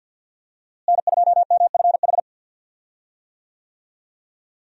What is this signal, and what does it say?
Morse code, heard as a keyed tone, about 51 wpm, 700 Hz. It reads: N2MFH
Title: USA Call Signs - 5 Characters - Repeated 2x - 50wpm